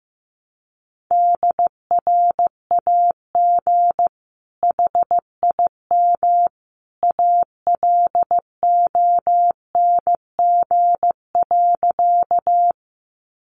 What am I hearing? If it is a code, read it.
Morse code, heard as a keyed tone, about 15 wpm, 700 Hz. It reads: DRAG HIM ALONG.
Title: Sentences from Top 1500 Words 15wpm